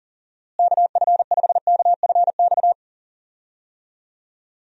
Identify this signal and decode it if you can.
Morse code, heard as a keyed tone, about 40 wpm, 700 Hz. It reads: KF5KFX
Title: USA Call Signs - 6 Characters - Repeated 3x - 40wpm